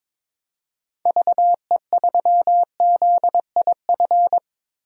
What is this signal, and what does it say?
Morse code, heard as a keyed tone, about 22 wpm, 700 Hz. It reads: VE3ZIF
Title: DXCC-100 Call Signs - 6 Characters - 22wpm